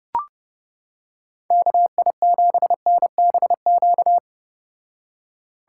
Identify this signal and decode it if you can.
Morse code, heard as a keyed tone, about 31 wpm, 700 Hz. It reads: KI7NBQ